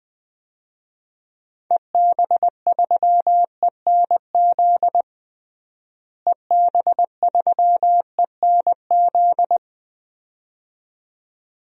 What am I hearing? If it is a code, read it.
Morse code, heard as a keyed tone, about 20 wpm, 700 Hz. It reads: EB3ENZ EB3ENZ
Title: DXCC-100 Call Signs - 6 Characters - Repeated 2x - 20wpm